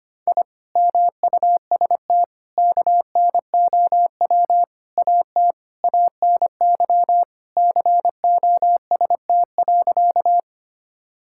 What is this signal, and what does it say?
Morse code, heard as a keyed tone, about 25 wpm, 700 Hz. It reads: I MUST KNOW AT ANY COST.
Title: Sentences from Top 1000 Words - Review Part 5 - 25wpm